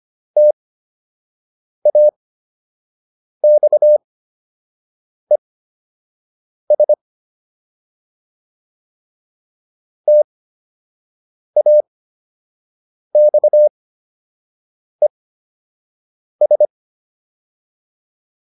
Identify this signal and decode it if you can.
Morse code, heard as a keyed tone, about 25 wpm, spaced as for 6 wpm, 600 Hz. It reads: TAXES TAXES